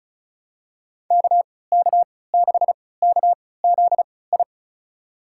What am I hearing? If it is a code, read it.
Morse code, heard as a keyed tone, about 36 wpm, spaced as for 20 wpm, 700 Hz. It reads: KK6KZI
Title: USA Call Signs - 6 Characters - 3x Character Spacing - 35wpm